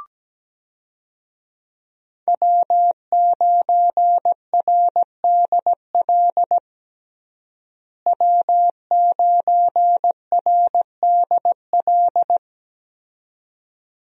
Morse code, heard as a keyed tone, about 17 wpm, 700 Hz. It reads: W9RDL W9RDL
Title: USA Call Signs - 5 Characters - Repeated 2x - 17wpm